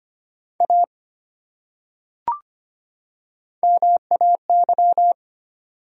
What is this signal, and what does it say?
Morse code, heard as a keyed tone, about 25 wpm, 700 Hz. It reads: A MAY
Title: Top 100 Words 25wpm